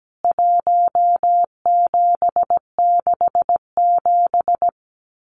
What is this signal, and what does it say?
Morse code, heard as a keyed tone, about 17 wpm, 700 Hz. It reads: 1767
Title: CWT Exchange 17wpm